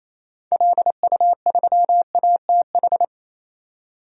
Morse code, heard as a keyed tone, about 28 wpm, 700 Hz. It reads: LU3ATH
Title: DXCC-100 Call Signs - 6 Characters - 28wpm